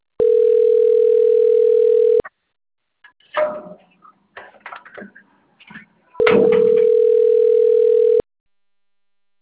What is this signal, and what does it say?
Ringback tone (North America), 440 + 480 Hz.